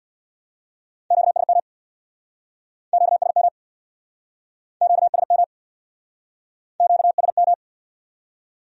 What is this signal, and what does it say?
Morse code, heard as a keyed tone, about 64 wpm, 700 Hz. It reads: QSM QSM QSM QSM